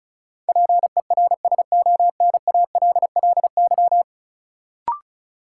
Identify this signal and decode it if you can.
Morse code, heard as a keyed tone, about 36 wpm, 700 Hz. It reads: PERSONALLY
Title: Sentences from Top 1400 Words - New Word Drill Alt - 35wpm